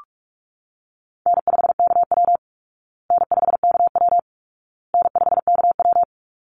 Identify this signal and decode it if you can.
Morse code, heard as a keyed tone, about 44 wpm, 700 Hz. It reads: N5KW N5KW N5KW
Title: Easy Call Signs Repeated 3x - 45wpm